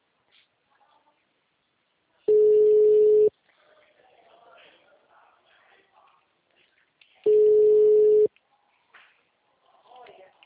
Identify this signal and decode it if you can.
Ringback tone (Europe, 425 Hz), with speech.